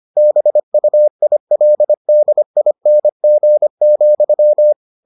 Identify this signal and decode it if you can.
Morse code, heard as a keyed tone, about 25 wpm, 600 Hz. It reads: BUILDING,